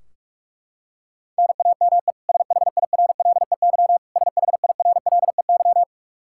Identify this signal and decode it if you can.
Morse code, heard as a keyed tone, about 46 wpm, 700 Hz. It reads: NAME SHIRLEY SHIRLEY
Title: K1USN SST QSO Practice 45wpm